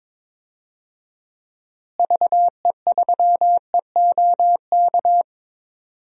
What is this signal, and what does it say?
Morse code, heard as a keyed tone, about 22 wpm, 700 Hz. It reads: VE3EOK
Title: DXCC-100 Call Signs - 6 Characters - Repeated 3x - 22wpm